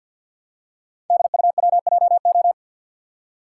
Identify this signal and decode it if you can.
Morse code, heard as a keyed tone, about 47 wpm, 690 Hz. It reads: DUWJO
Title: Code Groups - 5 Letter:Numbers - 50wpm